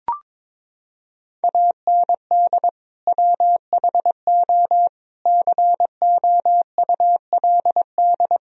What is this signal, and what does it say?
Morse code, heard as a keyed tone, about 22 wpm, 700 Hz. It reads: AND WHO COULD
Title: Sentences from Top 1000 Words - Review Part 3 - 22wpm